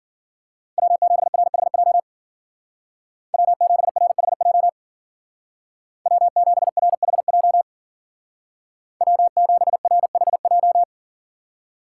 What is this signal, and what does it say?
Morse code, heard as a keyed tone, about 52 wpm, 700 Hz. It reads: W7RHJ W7RHJ W7RHJ W7RHJ